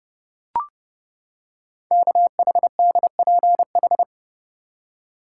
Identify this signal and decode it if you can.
Morse code, heard as a keyed tone, about 30 wpm, 700 Hz. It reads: KHDPH
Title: Code Groups - 5 Letters - 30wpm